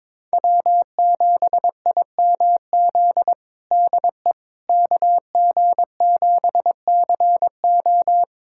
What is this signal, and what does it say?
Morse code, heard as a keyed tone, about 22 wpm, 700 Hz. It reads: W7IMZ DE KG7CO
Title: USA Callsign DE Callsign - 5 Characters - 22wpm